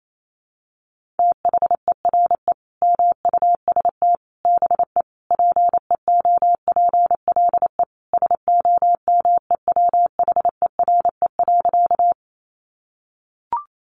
Morse code, heard as a keyed tone, about 28 wpm, 700 Hz. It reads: THERE MUST BE PEOPLE SOMEWHERE.
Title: Sentences from Top 1400 Words 28wpm